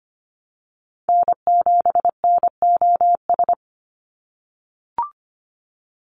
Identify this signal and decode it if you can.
Morse code, heard as a keyed tone, about 25 wpm, 700 Hz. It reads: N7NOS
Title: USA Call Signs 25wpm